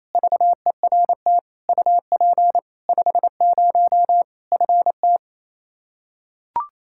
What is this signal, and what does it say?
Morse code, heard as a keyed tone, about 28 wpm, 700 Hz. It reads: VERT UP 50 FT
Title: Practice QSO - Exchange 2 - Info - 28wpm